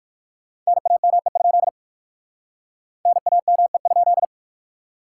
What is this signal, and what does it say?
Morse code, heard as a keyed tone, about 65 wpm, spaced as for 56 wpm, 700 Hz. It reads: NAME? NAME?